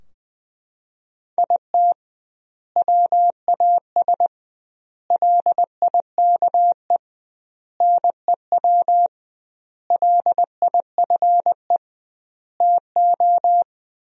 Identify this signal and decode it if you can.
Morse code, heard as a keyed tone, about 20 wpm, 700 Hz. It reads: IT WAS LIKE NEW LIFE TO
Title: 2x Word-Spacing - Sentences from Top 300 Words - 20wpm